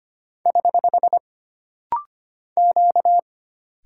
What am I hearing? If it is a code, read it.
Morse code, heard as a keyed tone, about 25 wpm, 700 Hz. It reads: <HH> Q
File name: Single Character (Letter:Number:Punctuation:Prosign) - Rapid-Fire - 25wpm